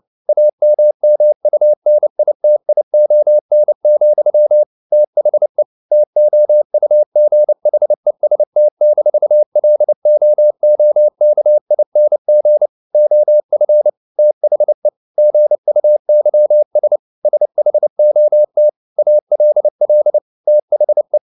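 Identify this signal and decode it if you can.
Morse code, heard as a keyed tone, about 29 wpm, 600 Hz. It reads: AMMUNITION, THE TOUGHEST-LOOKING OF THE GUYS SHOT ALL THE
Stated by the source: Morse code at 29 wpm, sending Ken Follett's novel Never in English, from CW med Gnister.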